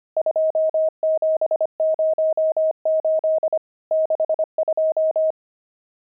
Morse code, heard as a keyed tone, about 25 wpm, 630 Hz. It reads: 2708 62
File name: pangram-fs-25-cs-25-nums12